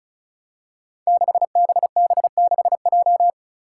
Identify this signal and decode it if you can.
Morse code, heard as a keyed tone, about 36 wpm, 700 Hz. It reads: 6BB6J
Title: Five Challenging Letter-Number Combinations 35wpm